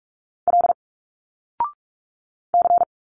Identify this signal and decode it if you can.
Morse code, heard as a keyed tone, about 45 wpm, 700 Hz. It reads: L C
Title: Single Letters 45wpm